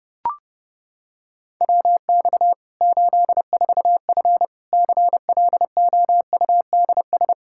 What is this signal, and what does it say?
Morse code, heard as a keyed tone, about 30 wpm, 700 Hz. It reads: WX 84F CLOUDS
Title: Practice QSO - Exchange 2 - Info - 30wpm